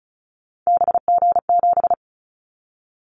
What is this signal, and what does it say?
Morse code, heard as a keyed tone, about 36 wpm, 700 Hz. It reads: BG7
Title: Callsign Prefixes for Top 100 DXCC - 35wpm